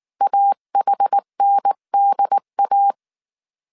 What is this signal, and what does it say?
Morse code, heard as a keyed tone, about 19 wpm, 800 Hz. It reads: AHNDA